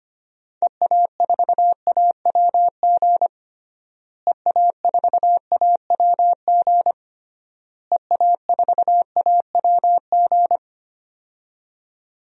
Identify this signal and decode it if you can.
Morse code, heard as a keyed tone, about 25 wpm, 700 Hz. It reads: EA4AWG EA4AWG EA4AWG
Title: DXCC-100 Call Signs - 6 Characters - Repeated 3x - 25wpm